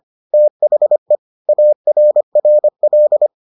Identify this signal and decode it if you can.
Morse code, heard as a keyed tone, about 25 wpm, 600 Hz. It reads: THE ARRL